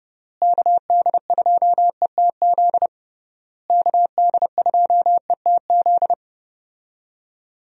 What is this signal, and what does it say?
Morse code, heard as a keyed tone, about 30 wpm, 700 Hz. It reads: KD2ETZ KD2ETZ
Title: USA Call Signs - 6 Characters - Repeated 3x - 30wpm